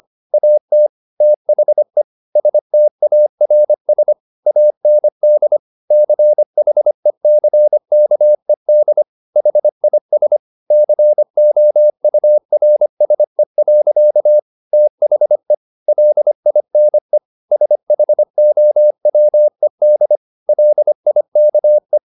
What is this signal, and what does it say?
Morse code, heard as a keyed tone, about 25 wpm, 600 Hz. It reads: AT THE STARS AND CHECKED HIS COURSE. THE LINE SHOWED LIKE
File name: the old man and the sea part 1 25wpm